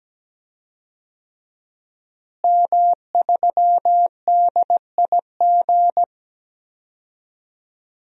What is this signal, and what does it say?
Morse code, heard as a keyed tone, about 17 wpm, 700 Hz. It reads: M3DIG
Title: DXCC-100 Call Signs - 5 Characters - Repeated 3x - 17wpm